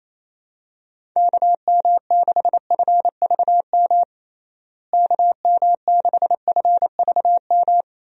Morse code, heard as a keyed tone, about 28 wpm, 700 Hz. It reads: KM6FVM KM6FVM
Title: USA Call Signs - 6 Characters - Repeated 2x - 28wpm